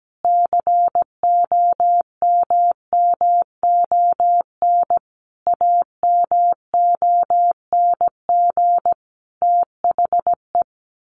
Morse code, heard as a keyed tone, about 17 wpm, 700 Hz. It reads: COMMON AMONG THE
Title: Sentences from Top 1400 Words 17wpm